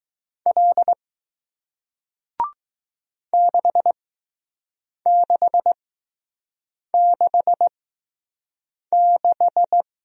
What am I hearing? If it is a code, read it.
Morse code, heard as a keyed tone, about 20 wpm, spaced as for 6 wpm, 700 Hz. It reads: L 6666